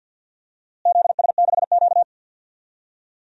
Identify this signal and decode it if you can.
Morse code, heard as a keyed tone, about 50 wpm, 700 Hz. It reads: ZS6Q
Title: DXCC-100 Call Signs - 4 Characters - Repeated 2x - 50wpm